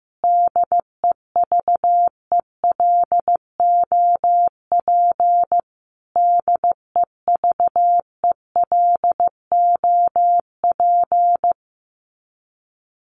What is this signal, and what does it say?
Morse code, heard as a keyed tone, about 15 wpm, 700 Hz. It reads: DEVELOP DEVELOP
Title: Top 500 Words Repeated 15wpm